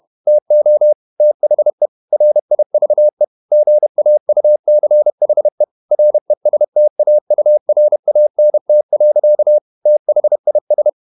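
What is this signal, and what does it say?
Morse code, heard as a keyed tone, about 31 wpm, 600 Hz. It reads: TO THE RIVE GAUCHE RESTAURANT. THIS
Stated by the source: Morse code at 30 wpm, the novel Never by Ken Follett in English, from CW med Gnister.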